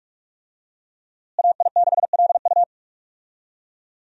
Morse code, heard as a keyed tone, about 46 wpm, 700 Hz. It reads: AI6LU
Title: USA Call Signs - 5 Characters - Repeated 3x - 45wpm